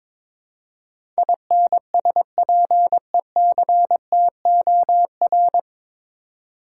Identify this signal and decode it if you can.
Morse code, heard as a keyed tone, about 22 wpm, 700 Hz. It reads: INSPECTOR
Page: Morse Code Ninja practice 22wpm